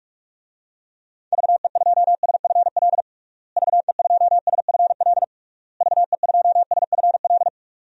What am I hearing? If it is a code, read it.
Morse code, heard as a keyed tone, about 42 wpm, 700 Hz. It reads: VE2SFL VE2SFL VE2SFL
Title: DXCC-100 Call Signs - 6 Characters - Repeated 3x - 45wpm